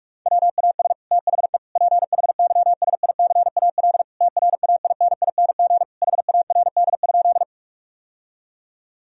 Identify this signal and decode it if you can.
Morse code, heard as a keyed tone, about 46 wpm, 700 Hz. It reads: WAS THE PHYSICAL TRAINING HARD?
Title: Sentences from Top 1000 Words - Review Part 4 - 45wpm